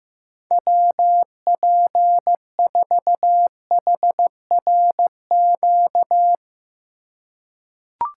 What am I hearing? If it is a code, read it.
Morse code, heard as a keyed tone, about 15 wpm, 700 Hz. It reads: WP4HRQ
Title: USA Call Signs - 6 Characters - 3x Character Spacing - 15wpm